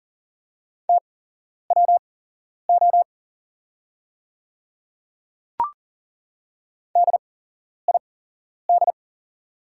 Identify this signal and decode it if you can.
Morse code, heard as a keyed tone, about 40 wpm, spaced as for 11 wpm, 690 Hz. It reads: TWO DID
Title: 8x Character Spacing - Top 100 Words - 40wpm